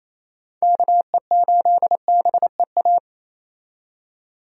Morse code, heard as a keyed tone, about 28 wpm, 700 Hz. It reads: KE8BEA
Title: USA Call Signs - 6 Characters - 28wpm